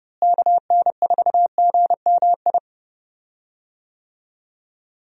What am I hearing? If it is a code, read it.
Morse code, heard as a keyed tone, about 30 wpm, 700 Hz. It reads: KN4GMI
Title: USA Call Signs - 6 Characters - Repeated 3x - 30wpm